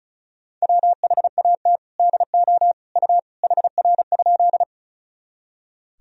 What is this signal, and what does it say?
Morse code, heard as a keyed tone, about 34 wpm, 700 Hz. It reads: WHAT DO U HR?
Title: Sentences from Top 300 Words and Abbreviations 35wpm